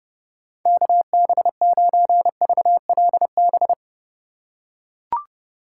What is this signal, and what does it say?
Morse code, heard as a keyed tone, about 32 wpm, 700 Hz. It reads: KB9VLB
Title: USA Call Signs - 6 Characters - 3x Character Spacing - 30wpm